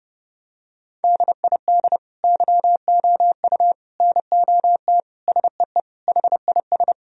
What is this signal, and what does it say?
Morse code, heard as a keyed tone, about 32 wpm, 700 Hz. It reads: DID YOU NOT SEE HIS
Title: Sentences from Top 1000 Words - Review Part 2 - 30wpm